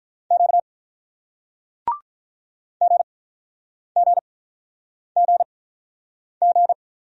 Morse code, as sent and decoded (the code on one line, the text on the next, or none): --.- / --. --. --. --.
Q GGGG